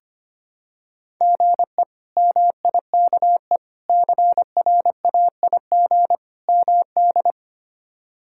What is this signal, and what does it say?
Morse code, heard as a keyed tone, about 25 wpm, 700 Hz. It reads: GE MIKE CRAIG MD
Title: K1USN SST QSO Practice - 25wpm